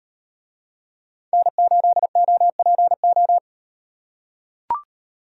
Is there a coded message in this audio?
Yes: Morse code from 1.3 to 3.4 s, heard as a keyed tone, about 37 wpm, 700 Hz. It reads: N8OPO